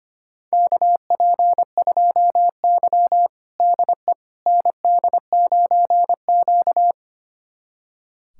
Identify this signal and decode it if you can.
Morse code, heard as a keyed tone, about 25 wpm, 700 Hz. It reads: KP2Y DE ND9Q
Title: USA Callsign DE Callsign - 4 Characters - 25wpm